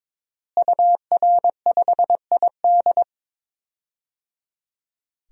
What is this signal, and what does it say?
Morse code, heard as a keyed tone, about 22 wpm, 700 Hz. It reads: UR5ID
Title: DXCC-100 Call Signs - 5 Characters - Repeated 3x - 22wpm